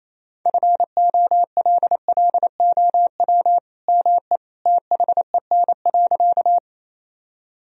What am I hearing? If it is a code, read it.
Morse code, heard as a keyed tone, about 28 wpm, 700 Hz. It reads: FOLLOW ME THEN.
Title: Sentences from Top 1000 Words - Review Part 1 - 28wpm